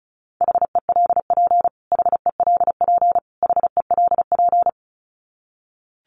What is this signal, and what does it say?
Morse code, heard as a keyed tone, about 34 wpm, 700 Hz. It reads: HELP HELP HELP
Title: Top 300 Words Repeated 35wpm